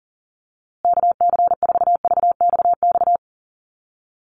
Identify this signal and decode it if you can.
Morse code, heard as a keyed tone, about 40 wpm, 700 Hz. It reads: KC4VXX
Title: USA Call Signs Repeated 2x 40wpm